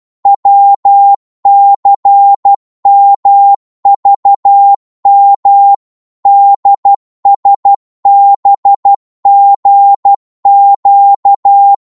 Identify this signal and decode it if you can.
Morse code, heard as a keyed tone, about 12 wpm, 800 Hz. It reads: WCMVM DSBGQ
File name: Morse / ltr-12wpm